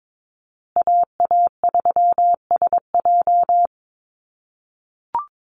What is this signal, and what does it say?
Morse code, heard as a keyed tone, about 22 wpm, 700 Hz. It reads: AA3SJ
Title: USA Call Signs Repeated 3x 22wpm